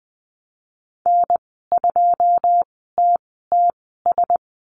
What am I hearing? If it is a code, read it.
Morse code, heard as a keyed tone, about 20 wpm, spaced as for 15 wpm, 700 Hz. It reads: N2TTS
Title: 088 Lesson 15g - Number 5 - Call Signs 20wpm